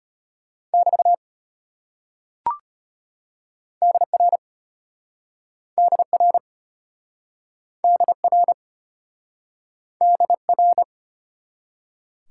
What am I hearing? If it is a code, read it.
Morse code, heard as a keyed tone, about 32 wpm, 700 Hz. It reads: <BT> DR DR DR DR